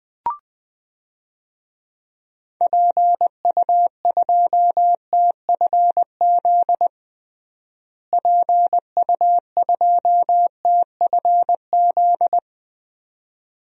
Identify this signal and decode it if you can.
Morse code, heard as a keyed tone, about 21 wpm, 700 Hz. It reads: PU2TFZ PU2TFZ
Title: DXCC-100 Call Signs - 6 Characters - Repeated 2x - 20wpm